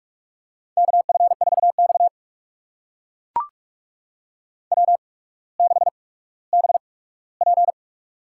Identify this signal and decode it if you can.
Morse code, heard as a keyed tone, about 44 wpm, spaced as for 19 wpm, 700 Hz. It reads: KF4X W6BP
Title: USA Call Signs - 4 Characters - 8x Character Spacing - 45wpm